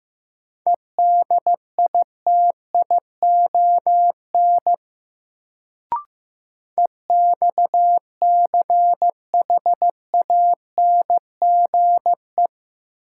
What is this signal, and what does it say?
Morse code, heard as a keyed tone, about 15 wpm, 700 Hz. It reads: EDITION EXCHANGE
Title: Sets of 5 Words - New Word Drill - 15wpm